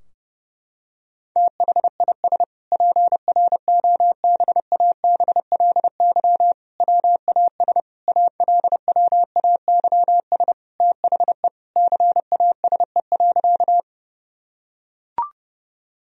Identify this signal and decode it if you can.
Morse code, heard as a keyed tone, about 32 wpm, 700 Hz. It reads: THIS PROBABLY WAS ALWAYS THE CASE.
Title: Sentences from Top 1000 Words - Review Part 4 - 30wpm